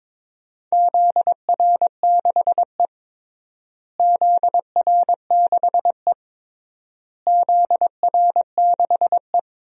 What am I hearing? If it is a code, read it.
Morse code, heard as a keyed tone, about 21 wpm, 700 Hz. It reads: ZR6E ZR6E ZR6E
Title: DXCC-100 Call Signs - 4 Characters - Repeated 3x - 22wpm